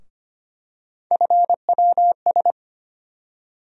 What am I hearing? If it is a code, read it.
Morse code, heard as a keyed tone, about 25 wpm, 700 Hz. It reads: FWS
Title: USA Call Sign Postfixes - 3 Letters 25wpm